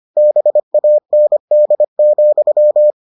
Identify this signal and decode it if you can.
Morse code, heard as a keyed tone, about 25 wpm, 600 Hz. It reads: BAND,